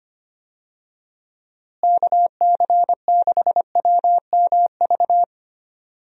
Morse code, heard as a keyed tone, about 25 wpm, 700 Hz. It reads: KC6WMV
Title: USA Call Signs - 6 Characters - Repeated 3x - 25wpm